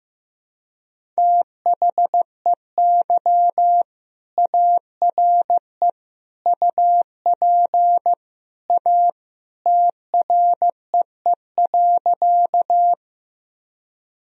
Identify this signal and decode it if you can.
Morse code, heard as a keyed tone, about 15 wpm, 700 Hz. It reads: THEY ARE UP A TREE.
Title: Sentences from Top 1000 Words - Review Part 1 - 15wpm